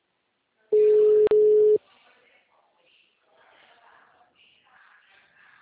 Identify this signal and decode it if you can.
Ringback tone (Europe, 425 Hz).
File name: out- -2002- - - .13274